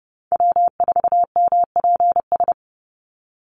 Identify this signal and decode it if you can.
Morse code, heard as a keyed tone, about 30 wpm, 700 Hz. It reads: W4MPS
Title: NA SOTA Chaser Callsigns 30wpm